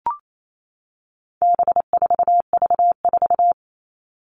Morse code, heard as a keyed tone, about 28 wpm, 700 Hz. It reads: B4V4
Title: All Cut Numbers - 4-Digit 28wpm